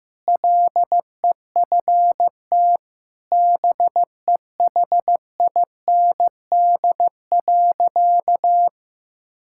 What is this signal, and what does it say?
Morse code, heard as a keyed tone, about 15 wpm, 700 Hz. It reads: LEFT BEHIND.
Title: Sentences from Top 1000 Words - Review Part 5 - 15wpm